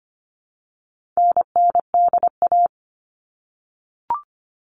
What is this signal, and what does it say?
Morse code, heard as a keyed tone, about 25 wpm, 700 Hz. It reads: NNDA
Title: All Cut Numbers - 4-Digit 25wpm